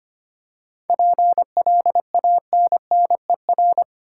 Morse code, heard as a keyed tone, about 25 wpm, 700 Hz. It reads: PLANNER
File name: practice / 25wpm / Difficult Words with Hints - 25wpm